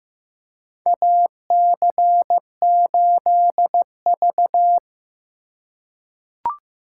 Morse code, heard as a keyed tone, about 15 wpm, 700 Hz. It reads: AC8V
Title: USA Call Signs - 8x Character Spacing - 15wpm